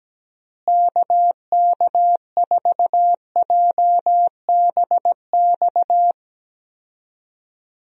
Morse code, heard as a keyed tone, about 17 wpm, 700 Hz. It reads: KK4JBX
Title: USA Call Signs - 6 Characters - Repeated 2x - 17wpm